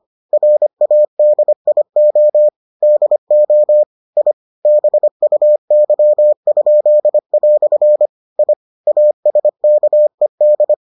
Morse code, heard as a keyed tone, about 25 wpm, 600 Hz. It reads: RADIO DO I BUY?" I ASKED